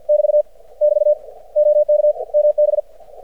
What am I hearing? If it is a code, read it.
Morse code, heard as a keyed tone, about 52 wpm, 600 Hz. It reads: <BT> <BT> OK MB
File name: archived / CRSnet